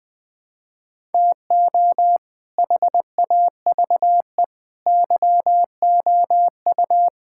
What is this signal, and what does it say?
Morse code, heard as a keyed tone, about 20 wpm, 700 Hz. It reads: TO HAVE YOU
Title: Build A Phrase - 5-Word Phrases - 20wpm